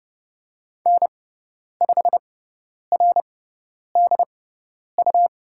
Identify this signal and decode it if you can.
Morse code, heard as a keyed tone, about 30 wpm, spaced as for 10 wpm, 700 Hz. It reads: N5RDU
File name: Lesson 18g - Letter m - Call Signs 30wpm